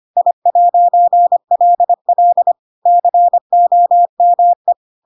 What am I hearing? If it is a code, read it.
Morse code, heard as a keyed tone, about 25 wpm, 700 Hz. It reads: I'LL COME